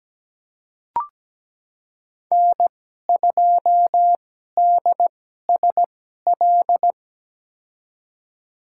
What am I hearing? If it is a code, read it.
Morse code, heard as a keyed tone, about 17 wpm, spaced as for 12 wpm, 700 Hz. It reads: N2DSL
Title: USA Call Signs - 2x Character Spacing - 17wpm